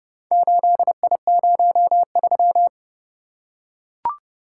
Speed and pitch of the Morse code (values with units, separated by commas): 28 wpm, 700 Hz